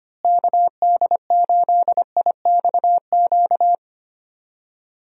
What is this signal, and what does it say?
Morse code, heard as a keyed tone, about 25 wpm, 700 Hz. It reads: KD8IXQ
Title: USA Call Signs - 6 Characters - Repeated 2x - 25wpm